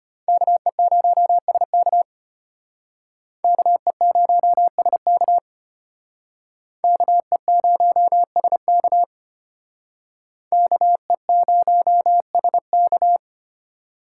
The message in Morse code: KE0SK KE0SK KE0SK KE0SK

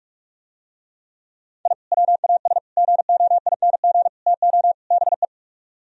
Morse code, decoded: I WAS GOING TO BE

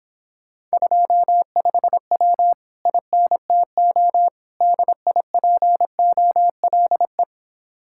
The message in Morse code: 25W INTO DIPOLE

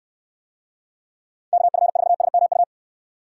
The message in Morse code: KF4SMU

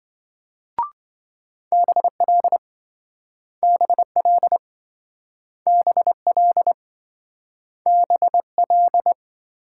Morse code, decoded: BL BL BL BL

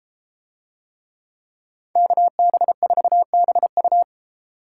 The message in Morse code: KB4BU

